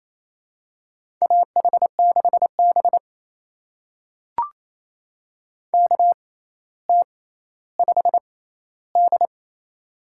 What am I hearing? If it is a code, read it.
Morse code, heard as a keyed tone, about 28 wpm, spaced as for 14 wpm, 700 Hz. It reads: AH6B KT5D
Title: USA Call Signs - 6x Character Spacing - 28wpm